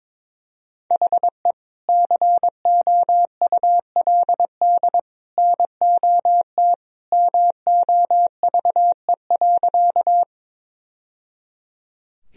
Morse code, heard as a keyed tone, about 22 wpm, 700 Hz. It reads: HE COULD NOT MOVE.
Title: Sentences from Top 1000 Words - Review Part 1 - 22wpm